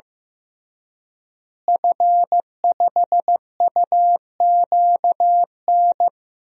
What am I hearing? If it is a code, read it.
Morse code, heard as a keyed tone, about 15 wpm, 700 Hz. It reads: F5UQN
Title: DXCC-100 Call Signs - 5 Characters - Repeated 3x - 15wpm